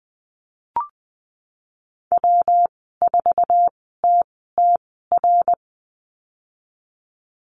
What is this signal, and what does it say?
Morse code, heard as a keyed tone, about 20 wpm, spaced as for 15 wpm, 700 Hz. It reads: W4TTR
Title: 147 Lesson 24g - Letter y - Call Signs 20wpm